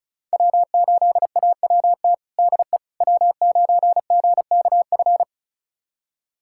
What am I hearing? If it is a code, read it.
Morse code, heard as a keyed tone, about 36 wpm, 700 Hz. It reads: W8AWT DE W9GKF
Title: USA Callsign DE Callsign - 5 Characters - 35wpm